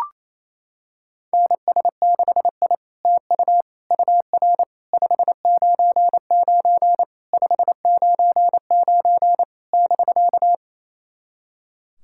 Morse code, heard as a keyed tone, about 29 wpm, 700 Hz. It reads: NS6I TU UR 599 599 <BK>
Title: SOTA QSO Practice 28wpm